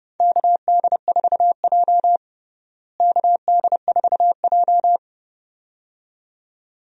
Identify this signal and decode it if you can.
Morse code, heard as a keyed tone, about 30 wpm, 700 Hz. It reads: KD4J KD4J
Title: USA Call Signs - 4 Characters - Repeated 2x - 30wpm